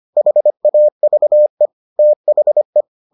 Morse code, heard as a keyed tone, about 25 wpm, 600 Hz. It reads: HAVE THE